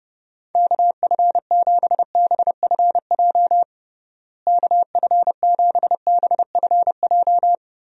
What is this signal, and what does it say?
Morse code, heard as a keyed tone, about 30 wpm, 700 Hz. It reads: KF7BFJ KF7BFJ
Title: USA Call Signs - 6 Characters - Repeated 2x - 30wpm